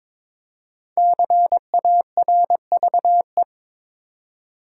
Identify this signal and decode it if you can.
Morse code, heard as a keyed tone, about 22 wpm, 700 Hz. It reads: CARVE
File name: 5 Letter Words - First 3 Letters Given - 22wpm